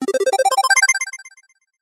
There is no code.